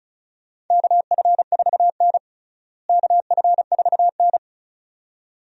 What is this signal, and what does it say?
Morse code, heard as a keyed tone, about 36 wpm, 700 Hz. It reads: KF4N KF4N
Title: USA Call Signs - 4 Characters - Repeated 2x - 35wpm